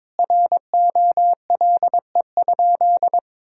Morse code, heard as a keyed tone, about 22 wpm, 700 Hz. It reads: ROLE?
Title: Sentences from Top 1000 Words - Review Part 6 - 22wpm